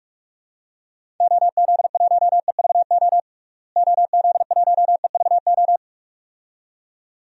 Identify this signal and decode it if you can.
Morse code, heard as a keyed tone, about 42 wpm, 700 Hz. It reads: OZ1EVO OZ1EVO
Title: DXCC-100 Call Signs - 6 Characters - Repeated 2x - 45wpm